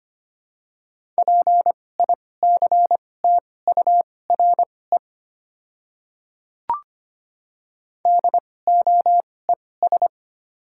Morse code, heard as a keyed tone, about 25 wpm, spaced as for 18 wpm, 690 Hz. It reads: PICTURE DOES
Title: 2x Character Spacing - Top 200 Words - 25wpm